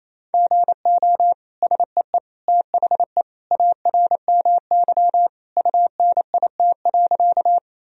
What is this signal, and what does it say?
Morse code, heard as a keyed tone, about 28 wpm, 700 Hz. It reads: GO SEE THE ARMY UNIT.
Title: Sentences from Top 1000 Words - Review Part 2 - 28wpm